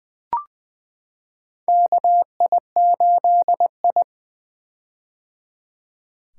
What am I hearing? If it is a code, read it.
Morse code, heard as a keyed tone, about 20 wpm, 700 Hz. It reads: KI8I
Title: K1USN SST QSO Practice - 20wpm